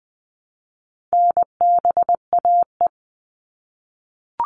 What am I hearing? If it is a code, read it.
Morse code, heard as a keyed tone, about 20 wpm, 700 Hz. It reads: NBAE
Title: All Cut Numbers - 4-Digit 20wpm